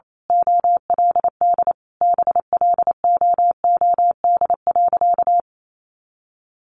Morse code, heard as a keyed tone, about 28 wpm, 700 Hz. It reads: OLD BLOOD.
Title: A Sound Of Thunder 28wpm